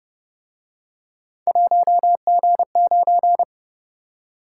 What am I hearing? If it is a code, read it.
Morse code, heard as a keyed tone, about 29 wpm, 700 Hz. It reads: 1G9
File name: Speed-Racing - Three Letter-Number Combinations - 25wpm